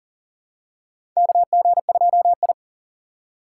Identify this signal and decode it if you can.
Morse code, heard as a keyed tone, about 40 wpm, 700 Hz. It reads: KG2I
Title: USA Call Signs - 4 Characters - 4x Character Spacing - 40wpm